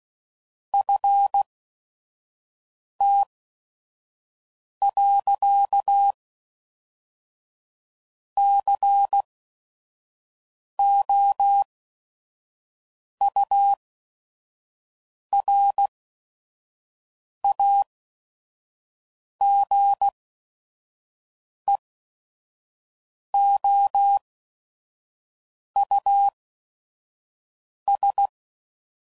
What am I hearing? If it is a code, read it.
Morse code, heard as a keyed tone, about 16 wpm, spaced as for 5 wpm, 800 Hz. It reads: FT.COURAGEOUS
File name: CW Code Groups @5WPM / fromSM's / Cities